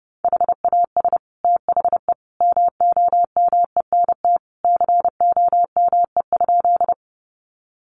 Morse code, heard as a keyed tone, about 30 wpm, 700 Hz. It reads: HAS THE MOMENT COME?
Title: Sentences from Top 1000 Words 30wpm